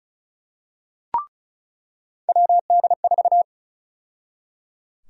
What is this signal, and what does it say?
Morse code, heard as a keyed tone, about 34 wpm, 700 Hz. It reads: WD4
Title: USA Call Sign Prefixes - 35wpm